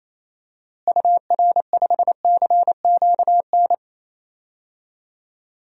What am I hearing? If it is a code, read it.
Morse code, heard as a keyed tone, about 28 wpm, 700 Hz. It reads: UR5CQN
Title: DXCC-100 Call Signs - 6 Characters - Repeated 3x - 28wpm